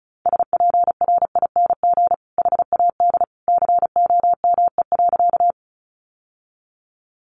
Morse code, heard as a keyed tone, about 34 wpm, 700 Hz. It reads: SPRING HAD COME.
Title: Sentences from Top 1200 Words 35wpm